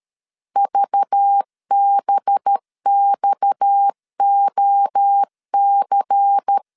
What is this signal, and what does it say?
Morse code, heard as a keyed tone, about 13 wpm, 800 Hz. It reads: VBXOC
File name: CW / DET / 1 6 1500C S A 12wpm 3440